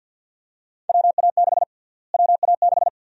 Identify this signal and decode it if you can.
Morse code, heard as a keyed tone, about 50 wpm, 700 Hz. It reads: WA6 WA6
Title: USA Call Sign Prefixes - Repeated 2x - 50wpm